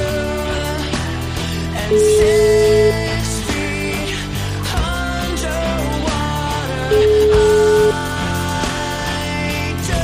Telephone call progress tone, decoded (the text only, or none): ringback tone